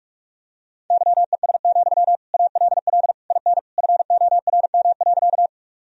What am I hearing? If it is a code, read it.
Morse code, heard as a keyed tone, about 44 wpm, 700 Hz. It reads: YES, ALL IN FORM.
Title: Sentences from Top 1000 Words - Review Part 1 - 45wpm